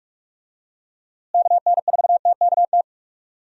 Morse code, heard as a keyed tone, about 45 wpm, 700 Hz. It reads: KN4TKT
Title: USA Call Signs - 6 Characters - Repeated 3x - 45wpm